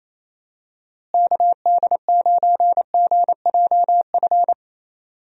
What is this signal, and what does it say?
Morse code, heard as a keyed tone, about 27 wpm, 700 Hz. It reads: KD9GJF
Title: USA Call Signs - 6 Characters - Repeated 2x - 28wpm